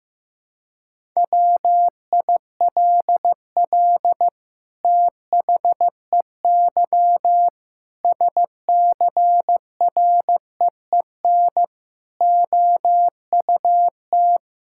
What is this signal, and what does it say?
Morse code, heard as a keyed tone, about 15 wpm, 700 Hz. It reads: WILL THEY SCREEN OUT
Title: Sentences from Top 1000 Words - Review Part 6 - 15wpm